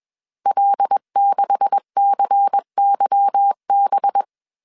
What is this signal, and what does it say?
Morse code, heard as a keyed tone, about 21 wpm, 800 Hz. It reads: L6CYB
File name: CW / DET / 1 9 1500C S C 20wpm 2310